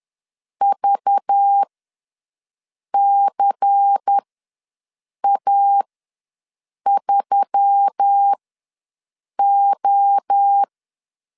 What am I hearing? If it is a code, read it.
Morse code, heard as a keyed tone, about 10 wpm, spaced as for 6 wpm, 800 Hz. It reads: VCA3O
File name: LEZIONI / 9 28 ws 10wpm 4124